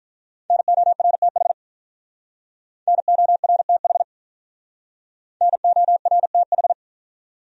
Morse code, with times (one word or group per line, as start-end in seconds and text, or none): 0.5-1.5 NORTH
2.9-4.0 NORTH
5.4-6.7 NORTH